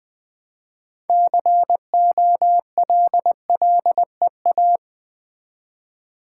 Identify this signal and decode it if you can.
Morse code, heard as a keyed tone, about 20 wpm, 700 Hz. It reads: COLLEA
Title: Build A Word - 3 Letters At A Time - 9-Letter Words - 20wpm